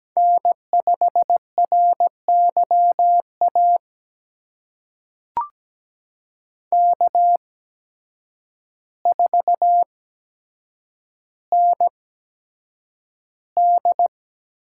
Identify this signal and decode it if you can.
Morse code, heard as a keyed tone, about 17 wpm, spaced as for 6 wpm, 700 Hz. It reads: N5RY AK4ND
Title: USA Call Signs - 8x Character Spacing - 17wpm